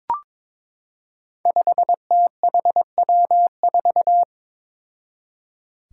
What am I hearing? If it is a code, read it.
Morse code, heard as a keyed tone, about 22 wpm, 700 Hz. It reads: 5THW4
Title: Five Challenging Letter-Number Combinations 22wpm